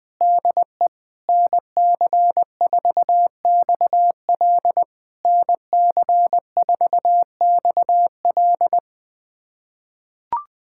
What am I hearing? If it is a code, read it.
Morse code, heard as a keyed tone, about 20 wpm, 700 Hz. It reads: DE NC4XL NC4XL
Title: POTA QSO Practice - 20wpm